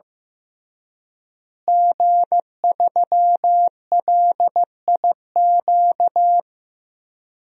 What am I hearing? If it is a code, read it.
Morse code, heard as a keyed tone, about 15 wpm, 700 Hz. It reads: G3LIQ
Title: DXCC-100 Call Signs - 5 Characters - Repeated 3x - 15wpm